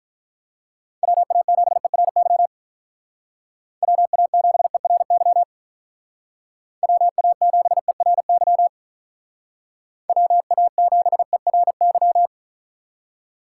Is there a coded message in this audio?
Yes: Morse code from 1.0 to 12.3 s, heard as a keyed tone, about 46 wpm, 700 Hz. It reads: WA7ERY WA7ERY WA7ERY WA7ERY